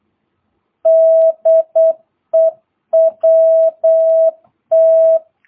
Morse code, heard as a keyed tone, about 8 wpm, 650 Hz. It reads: DE WT
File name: jh8cbh / 2022cw